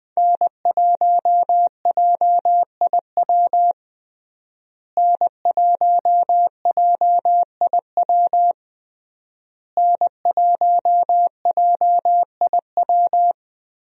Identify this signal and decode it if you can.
Morse code, heard as a keyed tone, about 20 wpm, 700 Hz. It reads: N1JIW N1JIW N1JIW
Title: USA Call Signs - 5 Characters - Repeated 3x - 20wpm